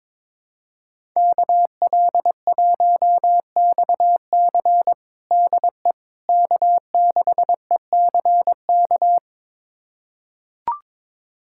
Morse code, heard as a keyed tone, about 22 wpm, 700 Hz. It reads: KL1XC DE K6ECK
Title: USA Callsign DE Callsign - 5 Characters - 22wpm